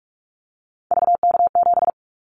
Morse code, heard as a keyed tone, about 44 wpm, 700 Hz. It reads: VK7
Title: Callsign Prefixes for Top 50 DXCC - 45wpm